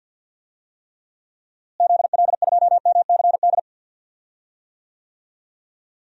Morse code, heard as a keyed tone, about 52 wpm, 700 Hz. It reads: ZL2MCD
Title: DXCC-100 Call Signs - 6 Characters - Repeated 2x - 50wpm